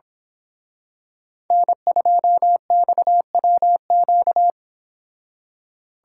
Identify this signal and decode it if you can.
Morse code, heard as a keyed tone, about 28 wpm, 700 Hz. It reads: N2XWQ